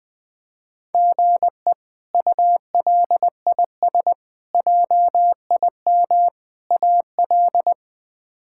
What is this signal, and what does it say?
Morse code, heard as a keyed tone, about 20 wpm, 700 Hz. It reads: GE ULIS JIM AL
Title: K1USN SST QSO Practice - 20wpm